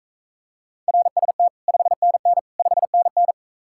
Morse code, heard as a keyed tone, about 45 wpm, 700 Hz. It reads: RST 5NN 5NN